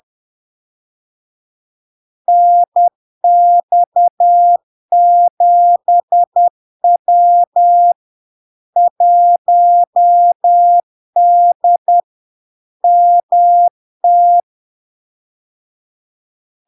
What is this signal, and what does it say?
Morse code, heard as a keyed tone, about 10 wpm, 700 Hz. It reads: NX7W 1D MT
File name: LARK-FD-2022-Log-in-CW-10wpm-700-Hz